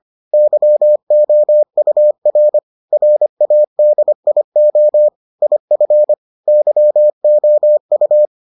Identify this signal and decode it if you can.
Morse code, heard as a keyed tone, about 25 wpm, 600 Hz. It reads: YOUR RADIO IF YOU